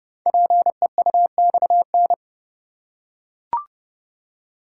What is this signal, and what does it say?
Morse code, heard as a keyed tone, about 30 wpm, 700 Hz. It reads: PEUXN
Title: Code Groups - 5 Letters - 30wpm